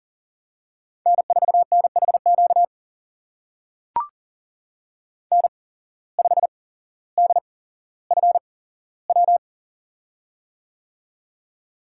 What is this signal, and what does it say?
Morse code, heard as a keyed tone, about 40 wpm, spaced as for 17 wpm, 690 Hz. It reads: N4NHQ N5DFW
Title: USA Call Signs - 8x Character Spacing - 40wpm